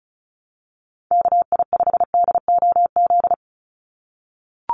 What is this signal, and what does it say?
Morse code, heard as a keyed tone, about 34 wpm, 700 Hz. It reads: KI5DOZ
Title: USA Call Signs 35wpm